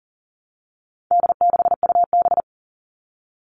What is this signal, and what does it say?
Morse code, heard as a keyed tone, about 40 wpm, 700 Hz. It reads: D6UB
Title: All Cut Numbers - 4-Digit 40wpm